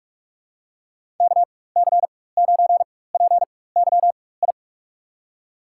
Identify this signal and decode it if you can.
Morse code, heard as a keyed tone, about 42 wpm, spaced as for 20 wpm, 700 Hz. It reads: KC9PYI
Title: USA Call Signs - 4x Character Spacing - 45wpm